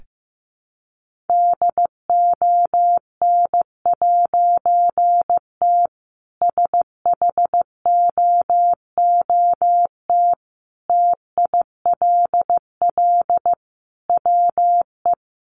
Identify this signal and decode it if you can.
Morse code, heard as a keyed tone, about 15 wpm, 700 Hz. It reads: DON'T SHOOT TILL WE
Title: A Sound Of Thunder 15wpm